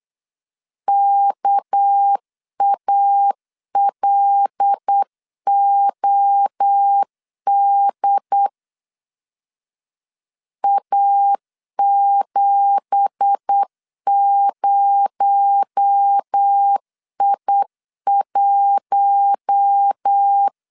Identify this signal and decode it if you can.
Morse code, heard as a keyed tone, about 8 wpm, 800 Hz. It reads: KALOD A70I1
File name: 8wpm